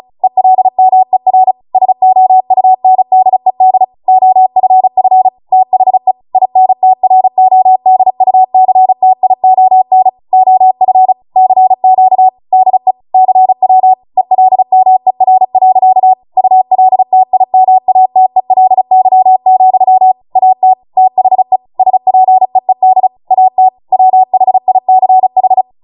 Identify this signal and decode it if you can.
Morse code, heard as a keyed tone, about 36 wpm, 750 Hz. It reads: ELMER SOUNDED OFF THE INTRODUCTION OF CQ DE CW ELMER. ULTIMATELY, AT THE SPEED AT WHICH